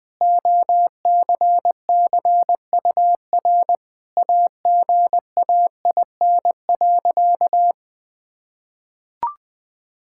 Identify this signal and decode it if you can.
Morse code, heard as a keyed tone, about 20 wpm, 700 Hz. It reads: OCCUR AGAIN.
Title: Sentences from Top 1000 Words - Review Part 6 - 20wpm